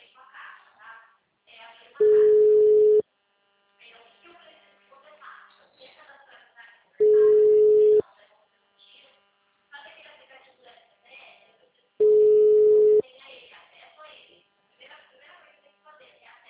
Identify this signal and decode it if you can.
Ringback tone (Europe, 425 Hz), with speech.